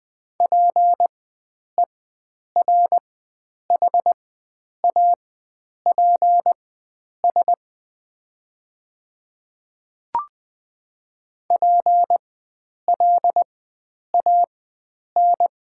Morse code, heard as a keyed tone, about 20 wpm, spaced as for 9 wpm, 700 Hz. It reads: PERHAPS PLAN
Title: 4x Character Spacing - Top 500 Words - 20wpm